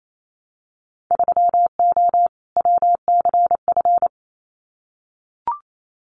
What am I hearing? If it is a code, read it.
Morse code, heard as a keyed tone, about 28 wpm, 700 Hz. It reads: 3O WCF
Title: Winter Field Day Exchanges 28wpm